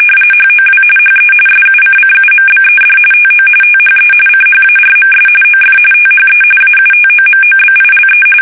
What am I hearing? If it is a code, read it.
RTTY, about 45.45 baud. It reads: HOW HE SUCCUMBED TO THE CHARMS OF THE AUTOMOBILE
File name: fsk850 / benjamin-g-lamme